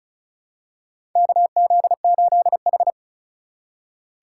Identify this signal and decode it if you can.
Morse code, heard as a keyed tone, about 36 wpm, 690 Hz. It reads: KZ8H
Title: USA Call Signs - 4 Characters - 2x Character Spacing - 35wpm